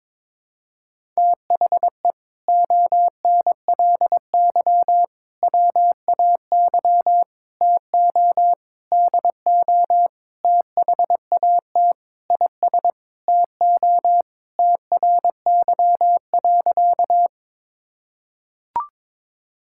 Morse code, heard as a keyed tone, about 22 wpm, 700 Hz. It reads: THE ONLY WAY TO DO THAT IS TO TRY.
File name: practice / 22wpm / Sentences with 8-Word Phrases - 22wpm